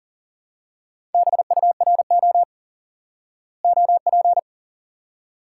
Morse code, heard as a keyed tone, about 40 wpm, 700 Hz. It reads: BURO OP
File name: IQE Course 3 - Lesson 89 - YAGI - Two Elements - 40wpm